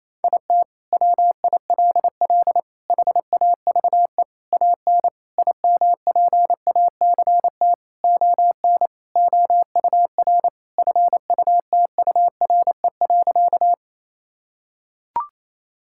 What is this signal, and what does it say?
Morse code, heard as a keyed tone, about 27 wpm, 700 Hz. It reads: IT WILL HAVE AN IMPACT ON OUR FUTURE.
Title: Sentences from Top 1000 Words - Review Part 6 - 28wpm